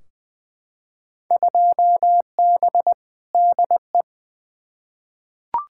Morse code, heard as a keyed tone, about 20 wpm, 700 Hz. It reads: 2B DE
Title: Field Day Exchanges Encore - 20wpm